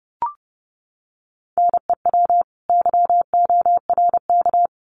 Morse code, heard as a keyed tone, about 30 wpm, 700 Hz. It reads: NEW YORK
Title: Sets of 2 Words 30wpm